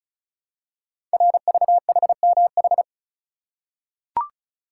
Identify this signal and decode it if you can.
Morse code, heard as a keyed tone, about 36 wpm, 700 Hz. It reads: RVHMH
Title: Code Groups - 5 Letters - 35wpm